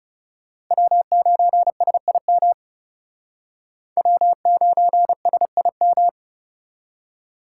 Morse code, heard as a keyed tone, about 37 wpm, 700 Hz. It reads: W9SIM W9SIM
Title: Speed Racing - USA 1x3 Call Signs 30wpm